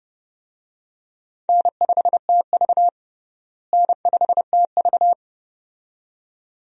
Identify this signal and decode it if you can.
Morse code, heard as a keyed tone, about 32 wpm, 700 Hz. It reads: N5TV N5TV
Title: USA Call Signs - 4 Characters - Repeated 2x - 30wpm